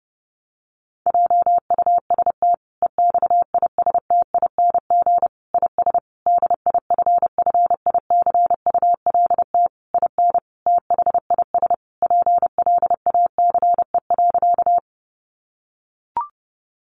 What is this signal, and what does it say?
Morse code, heard as a keyed tone, about 30 wpm, 700 Hz. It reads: JUST EXISTING IS DIFFICULT IN THIS PLACE.
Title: Sentences from Top 1500 Words 30wpm